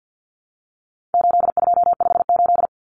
Morse code, heard as a keyed tone, about 50 wpm, 700 Hz. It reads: 8258
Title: Common Cut Numbers - 4-Digit 50wpm